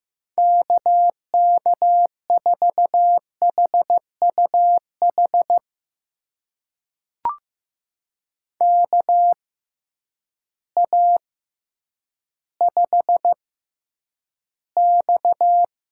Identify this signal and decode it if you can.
Morse code, heard as a keyed tone, about 15 wpm, spaced as for 9 wpm, 700 Hz. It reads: KK4HUH KA5X
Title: USA Call Signs - 6x Character Spacing - 15wpm